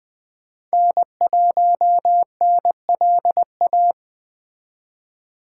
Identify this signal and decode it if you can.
Morse code, heard as a keyed tone, about 20 wpm, 700 Hz. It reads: N1NLA